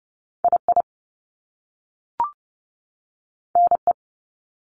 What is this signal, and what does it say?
Morse code, heard as a keyed tone, about 30 wpm, 700 Hz. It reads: II NE
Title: Top Two Letter Combinations in Words 30wpm